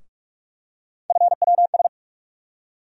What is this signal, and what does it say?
Morse code, heard as a keyed tone, about 43 wpm, 700 Hz. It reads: FWS
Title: USA Call Sign Postfixes - 3 Letters 45wpm